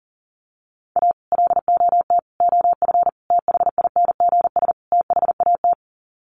Morse code, heard as a keyed tone, about 40 wpm, 700 Hz. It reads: A LOT OF THINGS THAT
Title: Sets of 5 Words 40wpm